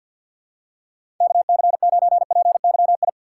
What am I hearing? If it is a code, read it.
Morse code, heard as a keyed tone, about 53 wpm, 700 Hz. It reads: KC9PYI
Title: USA Call Signs - 6 Characters - 4x Character Spacing - 50wpm